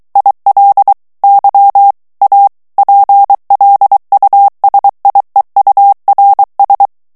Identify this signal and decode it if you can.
Morse code, heard as a keyed tone, about 23 wpm, 800 Hz. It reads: IL Y A PLUSIEURS
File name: FRANCAIS TEXTE CLAIR 1 VITESSE 22 MOTS MINUTES